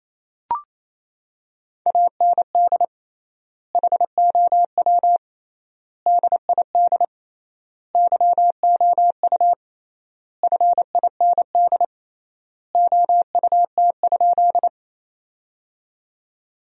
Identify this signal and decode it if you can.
Morse code, heard as a keyed tone, about 28 wpm, 700 Hz. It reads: AND HOW DID YOU FIND OUT?
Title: 3x Word-Spacing - Sentences from Top 100 Words - 28wpm